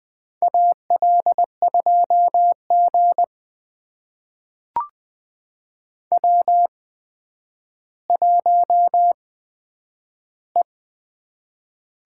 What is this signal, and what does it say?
Morse code, heard as a keyed tone, about 20 wpm, spaced as for 10 wpm, 700 Hz. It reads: AL2G W1E
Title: USA Call Signs - 4 Characters - 8x Character Spacing - 20wpm